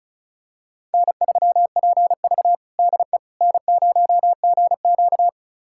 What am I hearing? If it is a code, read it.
Morse code, heard as a keyed tone, about 34 wpm, 700 Hz. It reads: N3PV DE N0GQ
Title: USA Callsign DE Callsign - 4 Characters - 35wpm